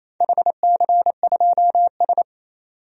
Morse code, heard as a keyed tone, about 28 wpm, 700 Hz. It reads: HC2S
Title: DXCC-100 Call Signs - 4 Characters - 28wpm